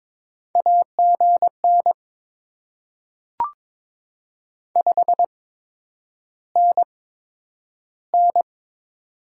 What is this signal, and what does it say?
Morse code, heard as a keyed tone, about 22 wpm, spaced as for 9 wpm, 700 Hz. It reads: AGN 5NN
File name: Top 20 QSO Elements - 8x Character Spacing - 22wpm